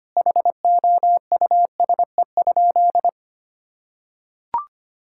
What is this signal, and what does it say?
Morse code, heard as a keyed tone, about 25 wpm, 700 Hz. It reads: HOUSE?
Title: Sentences from Top 1600 Words 25wpm